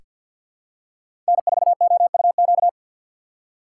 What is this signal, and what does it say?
Morse code, heard as a keyed tone, about 46 wpm, 700 Hz. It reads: N4OUQ
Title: USA Call Signs - 4x Character Spacing - 50wpm